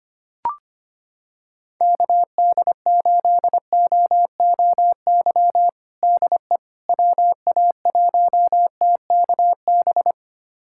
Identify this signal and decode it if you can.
Morse code, heard as a keyed tone, about 25 wpm, 700 Hz. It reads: KD8OOY DE WA1TKB
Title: USA Callsign DE Callsign - 6 Characters - 25wpm